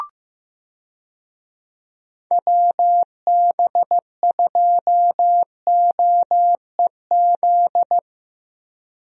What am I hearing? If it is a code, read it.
Morse code, heard as a keyed tone, about 15 wpm, 700 Hz. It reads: WB2OEZ
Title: USA Call Signs - 6 Characters - Repeated 2x - 15wpm